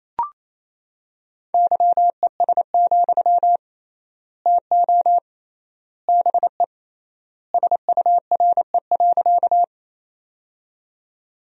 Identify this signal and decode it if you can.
Morse code, heard as a keyed tone, about 28 wpm, 700 Hz. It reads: YES, TO BE SURE.
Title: 3x Word-Spacing - Sentences from Top 500 Words - 28wpm